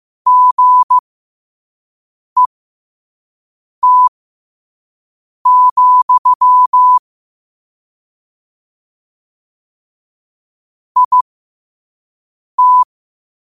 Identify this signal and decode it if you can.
Morse code, heard as a keyed tone, about 15 wpm, spaced as for 5 wpm, 1000 Hz. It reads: GET, IT
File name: qotdfiles / 5wpm